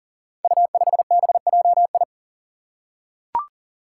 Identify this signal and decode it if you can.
Morse code, heard as a keyed tone, about 40 wpm, 700 Hz. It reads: U5BJI
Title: Five Challenging Letter-Number Combinations 40wpm